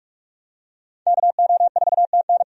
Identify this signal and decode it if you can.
Morse code, heard as a keyed tone, about 43 wpm, 700 Hz. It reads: KO4TN
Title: USA Call Signs - 5 Characters - 45wpm